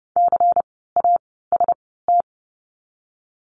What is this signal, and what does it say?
Morse code, heard as a keyed tone, about 31 wpm, 700 Hz. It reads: C A S T
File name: Lesson 16f - Letter c - Words 30wpm